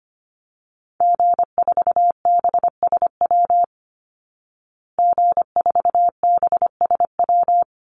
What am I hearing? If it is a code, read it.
Morse code, heard as a keyed tone, about 25 wpm, 700 Hz. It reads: G4BSW G4BSW
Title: Call Signs Repeated 3x - 25wpm